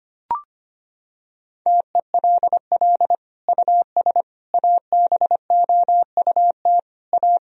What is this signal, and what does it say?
Morse code, heard as a keyed tone, about 25 wpm, 700 Hz. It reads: TELL US ABOUT A
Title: Sentences from Top 1000 Words - Review Part 3 - 25wpm